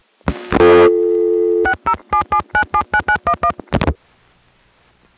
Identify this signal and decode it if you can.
DTMF, heard as telephone touch tones, tones of about 80 ms. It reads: 6077979911